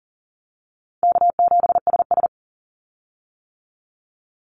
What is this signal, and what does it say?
Morse code, heard as a keyed tone, about 40 wpm, 700 Hz. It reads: K7SS
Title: Easy Call Signs - 40wpm